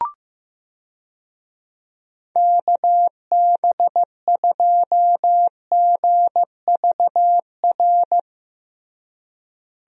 Morse code, heard as a keyed tone, about 15 wpm, 700 Hz. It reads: KB2GVR
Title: USA Call Signs - 6 Characters - Repeated 3x - 15wpm